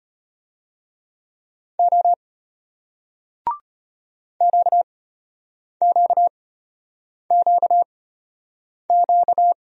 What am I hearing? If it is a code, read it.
Morse code, heard as a keyed tone, about 33 wpm, spaced as for 8 wpm, 700 Hz. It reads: O QQQQ